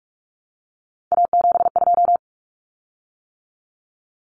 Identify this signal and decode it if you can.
Morse code, heard as a keyed tone, about 44 wpm, 700 Hz. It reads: A72
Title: Common Cut Numbers - 3-Digit 45wpm